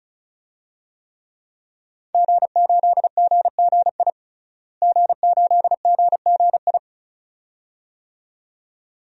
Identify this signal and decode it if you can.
Morse code, heard as a keyed tone, about 34 wpm, 700 Hz. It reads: G8GGI G8GGI
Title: DXCC-100 Call Signs - 5 Characters - Repeated 2x - 35wpm